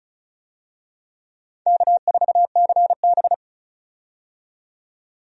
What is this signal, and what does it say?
Morse code, heard as a keyed tone, about 35 wpm, 700 Hz. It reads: K4CB
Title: USA Call Signs - 4 Characters - 35wpm